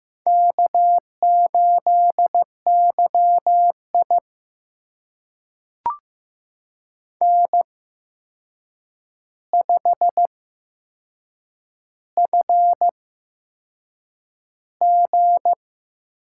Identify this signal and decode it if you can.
Morse code, heard as a keyed tone, about 15 wpm, spaced as for 6 wpm, 700 Hz. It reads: K8YI N5FG
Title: USA Call Signs - 8x Character Spacing - 15wpm